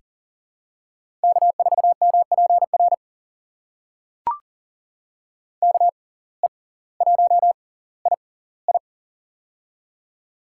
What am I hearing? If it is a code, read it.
Morse code, heard as a keyed tone, about 40 wpm, spaced as for 20 wpm, 700 Hz. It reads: K4MPR KE1II
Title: USA Call Signs - 6x Character Spacing - 40wpm